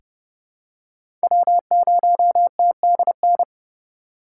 Morse code, heard as a keyed tone, about 31 wpm, 700 Hz. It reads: W0TDN
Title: USA Call Signs - 5 Characters - 4x Character Spacing - 30wpm